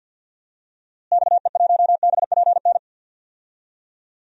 Morse code, heard as a keyed tone, about 53 wpm, 700 Hz. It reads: XE1BPN
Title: DXCC-100 Call Signs - 6 Characters - Repeated 2x - 50wpm